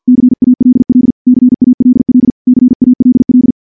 Morse code, heard as a keyed tone, about 49 wpm, 250 Hz. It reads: CALL CALL CALL
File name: morse / morse-training-50WPM